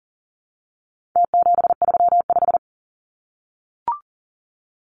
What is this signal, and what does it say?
Morse code, heard as a keyed tone, about 40 wpm, 700 Hz. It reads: T735
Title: Numbers - Contest Style - 4-Digit 40wpm